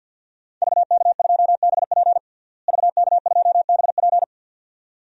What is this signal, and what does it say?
Morse code, heard as a keyed tone, about 47 wpm, 700 Hz. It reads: VK2BP VK2BP
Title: DXCC-100 Call Signs - 5 Characters - Repeated 2x - 50wpm